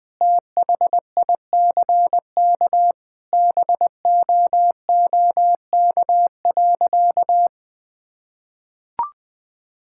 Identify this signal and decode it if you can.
Morse code, heard as a keyed tone, about 21 wpm, 700 Hz. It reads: THICK BOOK.
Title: Sentences with 4-Word Phrases - Spoken Beginning - 20wpm